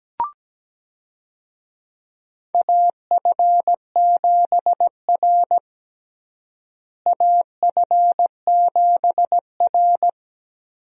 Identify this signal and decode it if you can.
Morse code, heard as a keyed tone, about 17 wpm, 700 Hz. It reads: AF7R AF7R
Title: USA Call Signs - 4 Characters - Repeated 2x - 17wpm